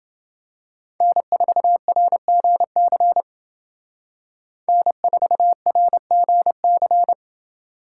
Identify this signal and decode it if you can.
Morse code, heard as a keyed tone, about 30 wpm, 700 Hz. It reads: N4RGC N4RGC